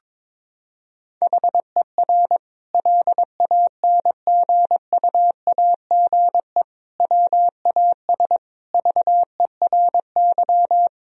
Morse code, heard as a keyed tone, about 22 wpm, 700 Hz. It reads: HER LANGUAGE WAS VERY
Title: Sentences from Top 1000 Words - Review Part 3 - 22wpm